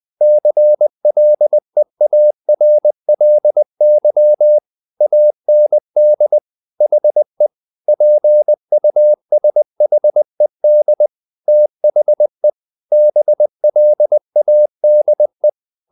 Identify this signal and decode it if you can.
Morse code, heard as a keyed tone, about 20 wpm, 600 Hz. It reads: CLEARLY AND HE PUSHED THE BLADE
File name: the old man and the sea part 2 20wpm